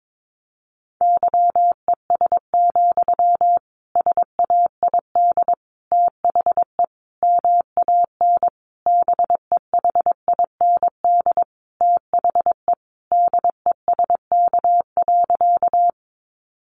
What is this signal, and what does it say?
Morse code, heard as a keyed tone, about 22 wpm, 700 Hz. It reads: YES, SAID THE MAN BEHIND THE DESK.
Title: A Sound Of Thunder 22wpm